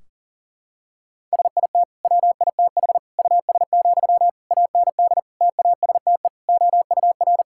Morse code, heard as a keyed tone, about 40 wpm, 700 Hz. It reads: SIT WITH US, AND TASTE OUR
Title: Sentences from Top 1600 Words 40wpm